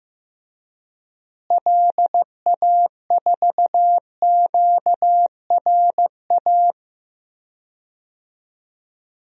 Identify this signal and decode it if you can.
Morse code, heard as a keyed tone, about 15 wpm, 700 Hz. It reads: LA4QRA
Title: DXCC-100 Call Signs - 6 Characters - Repeated 3x - 15wpm